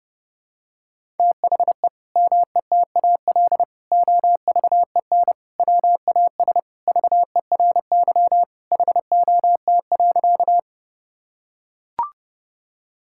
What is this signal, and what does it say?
Morse code, heard as a keyed tone, about 30 wpm, 700 Hz. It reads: THE METAL OVEN WAS VERY HOT.
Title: Sentences from Top 1000 Words - Review Part 6 - 30wpm